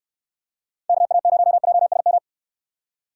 Morse code, heard as a keyed tone, about 67 wpm, 700 Hz. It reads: KA0JSM